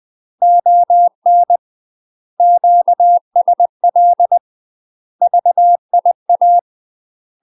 Morse code, heard as a keyed tone, about 20 wpm, 700 Hz. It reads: ON QSL VIA
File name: UK MASTER QSOs TRAINING / 20WPM / UK MASTER QSOs 0055